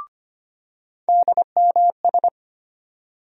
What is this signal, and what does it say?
Morse code, heard as a keyed tone, about 25 wpm, 700 Hz. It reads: DMS
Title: USA Call Sign Postfixes - 3 Letters 25wpm